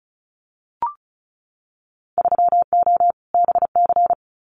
Morse code, heard as a keyed tone, about 36 wpm, 700 Hz. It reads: 3O BC
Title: Winter Field Day Exchanges Encore 35wpm